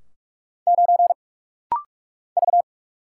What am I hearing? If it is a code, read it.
Morse code, heard as a keyed tone, about 44 wpm, 700 Hz. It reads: 9 V
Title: Single Letter-Number Alt - Rapid-Fire 45wpm